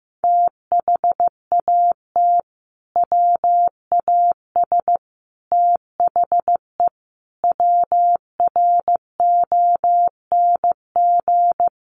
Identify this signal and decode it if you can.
Morse code, heard as a keyed tone, about 15 wpm, 700 Hz. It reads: THAT WAS THE WRONG
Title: Sentences from Top 900 Words 15wpm